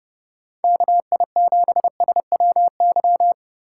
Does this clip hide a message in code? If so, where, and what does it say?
Yes: Morse code from 0.6 to 3.3 s, heard as a keyed tone, about 30 wpm, 700 Hz. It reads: KI7SWY